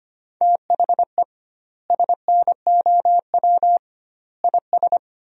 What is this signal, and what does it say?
Morse code, heard as a keyed tone, about 25 wpm, 700 Hz. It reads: THE SNOW IS
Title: 2x Word-Spacing - Sentences from Top 500 Words - 25wpm